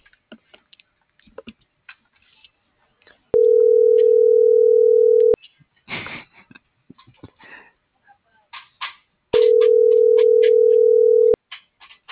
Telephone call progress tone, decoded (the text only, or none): ringback tone